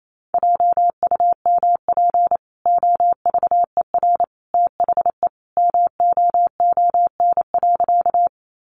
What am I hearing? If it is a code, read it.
Morse code, heard as a keyed tone, about 28 wpm, 700 Hz. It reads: JUMP OVER THE MOON.
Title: Sentences from Top 1300 Words 28wpm